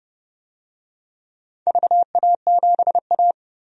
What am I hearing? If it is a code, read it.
Morse code, heard as a keyed tone, about 31 wpm, 700 Hz. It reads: VA7A
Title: DXCC-100 Call Signs - 4 Characters - 30wpm